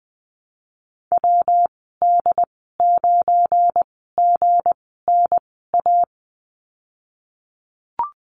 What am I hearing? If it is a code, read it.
Morse code, heard as a keyed tone, about 20 wpm, spaced as for 15 wpm, 700 Hz. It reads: WD9GNA
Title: 186 Lesson 30g - Letter b - Call Signs 20wpm